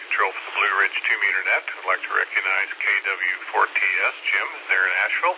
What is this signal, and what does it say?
Emergency Alert System (SAME) tones (header bursts), heard as harsh buzzing bursts, with speech. It is heard from 0.7 to 5.2 s.